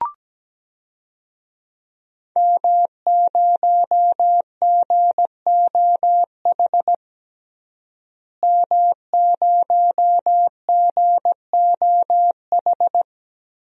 Morse code, heard as a keyed tone, about 17 wpm, 700 Hz. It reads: M0GOH M0GOH
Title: DXCC-100 Call Signs - 5 Characters - Repeated 2x - 17wpm